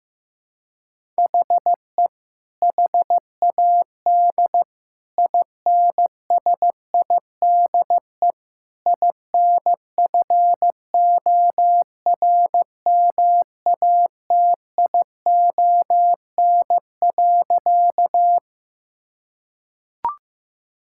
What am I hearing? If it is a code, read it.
Morse code, heard as a keyed tone, about 15 wpm, 700 Hz. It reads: HE HAD INSIDE INFORMATION.
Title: Sentences from Top 1000 Words - Review Part 6 - 15wpm